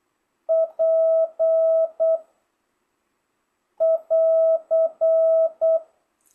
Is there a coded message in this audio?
Yes: Morse code from 0.5 to 5.8 s, heard as a keyed tone, about 8 wpm, 660 Hz. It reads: P <AR>